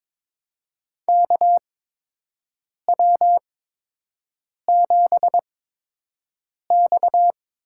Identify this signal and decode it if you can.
Morse code, heard as a keyed tone, about 21 wpm, spaced as for 6 wpm, 700 Hz. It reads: KW7X
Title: USA Call Signs - 5 Characters - 8x Character Spacing - 22wpm